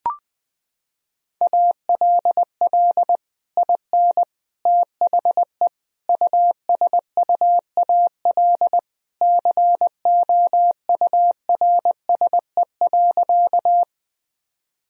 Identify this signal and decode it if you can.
Morse code, heard as a keyed tone, about 20 wpm, 700 Hz. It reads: ALL IN THE USUAL COURSE.
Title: Sentences from Top 1000 Words - Review Part 2 - 20wpm